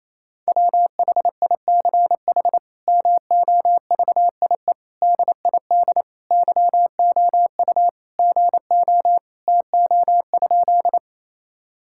Morse code, heard as a keyed tone, about 28 wpm, 700 Hz. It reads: WHICH MOVIE DID YOU GO TO?
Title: Sentences from Top 1000 Words - Review Part 5 - 28wpm